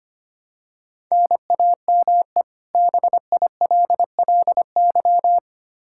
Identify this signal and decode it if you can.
Morse code, heard as a keyed tone, about 25 wpm, 700 Hz. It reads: NAME BILLY
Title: Practice QSO - Exchange 1 - Beginning - 25wpm